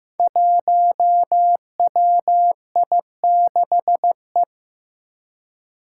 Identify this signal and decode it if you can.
Morse code, heard as a keyed tone, about 15 wpm, 700 Hz. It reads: 1WI6E
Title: Five Challenging Letter-Number Combinations 15wpm